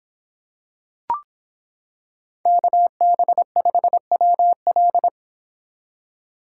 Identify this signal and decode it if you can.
Morse code, heard as a keyed tone, about 26 wpm, 700 Hz. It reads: KB5WL